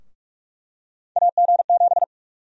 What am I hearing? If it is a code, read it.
Morse code, heard as a keyed tone, about 43 wpm, 700 Hz. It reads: AG7
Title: USA Call Sign Prefixes - 45wpm